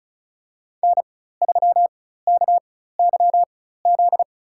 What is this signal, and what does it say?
Morse code, heard as a keyed tone, about 36 wpm, spaced as for 16 wpm, 700 Hz. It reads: N3KYZ
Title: USA Call Signs - 5 Characters - 4x Character Spacing - 35wpm